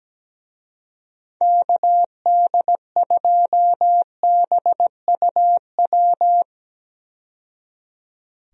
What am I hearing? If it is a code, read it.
Morse code, heard as a keyed tone, about 17 wpm, 700 Hz. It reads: KD2BUW
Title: USA Call Signs - 6 Characters - 17wpm